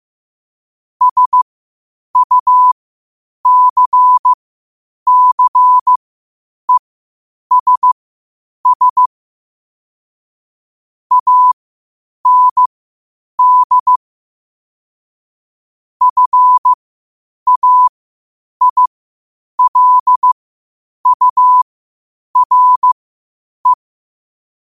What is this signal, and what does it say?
Morse code, heard as a keyed tone, about 15 wpm, spaced as for 8 wpm, 1000 Hz. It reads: SUCCESS AND FAILURE
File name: 8wpm